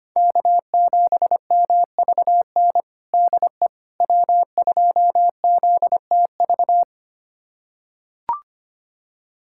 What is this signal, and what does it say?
Morse code, heard as a keyed tone, about 25 wpm, 700 Hz. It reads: K7MVN DE W2ZTV
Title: USA Callsign DE Callsign 25wpm